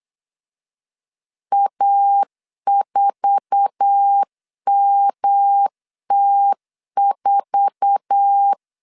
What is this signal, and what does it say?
Morse code, heard as a keyed tone, about 8 wpm, 800 Hz. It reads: A4MT4
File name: LEZIONI / 8 YJ45 / 8 YJ4 8wpm 3246